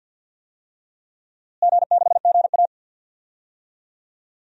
Morse code, heard as a keyed tone, about 52 wpm, 700 Hz. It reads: G6GA
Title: DXCC-100 Call Signs - 4 Characters - Repeated 3x - 50wpm